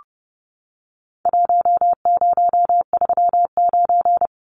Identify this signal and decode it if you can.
Morse code, heard as a keyed tone, about 30 wpm, 700 Hz. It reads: 1039
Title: Numbers - Contest Style - 4-Digit 30wpm